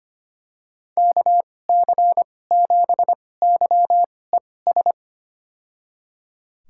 Morse code, heard as a keyed tone, about 25 wpm, spaced as for 18 wpm, 700 Hz. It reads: KC7YES
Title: USA Call Signs - 6 Characters - 2x Character Spacing - 25wpm